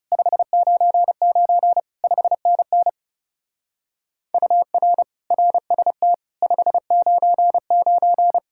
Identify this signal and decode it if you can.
Morse code, heard as a keyed tone, about 37 wpm, 700 Hz. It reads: 599 5NN UR RST 599
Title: Speed-Racing Common QSO Phrases Encore - 30wpm